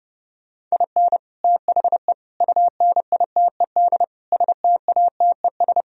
Morse code, heard as a keyed tone, about 33 wpm, 700 Hz. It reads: IN THE UNITED STATES